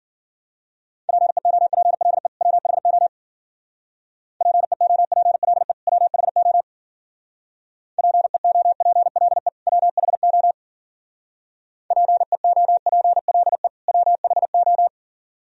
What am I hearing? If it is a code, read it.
Morse code, heard as a keyed tone, about 49 wpm, 700 Hz. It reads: PEOPLE WHO PEOPLE WHO PEOPLE WHO PEOPLE WHO
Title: Speed-Racing - Sets of 2 Words 40wpm